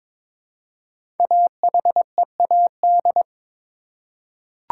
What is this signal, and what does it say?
Morse code, heard as a keyed tone, about 22 wpm, 690 Hz. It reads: AHEAD
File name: 5 Letter Words - First 3 Letters Given - 22wpm